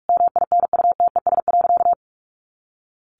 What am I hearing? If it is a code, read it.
Morse code, heard as a keyed tone, about 44 wpm, 700 Hz. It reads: MINUTES.
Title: A Sound Of Thunder 45wpm